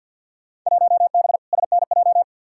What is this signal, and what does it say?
Morse code, heard as a keyed tone, about 47 wpm, 700 Hz. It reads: 1B SNJ
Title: Field Day Exchanges Encore - 50wpm